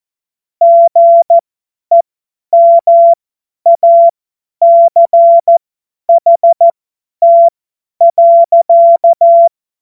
Morse code, heard as a keyed tone, about 14 wpm, spaced as for 10 wpm, 680 Hz. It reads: GEMACHT.
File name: Morsekurs / Witze / Witz 15. November 2022 50 F70